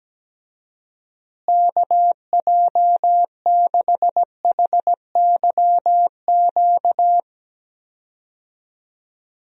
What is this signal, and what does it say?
Morse code, heard as a keyed tone, about 17 wpm, 700 Hz. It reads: KJ6HYQ
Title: USA Call Signs - 6 Characters - Repeated 2x - 17wpm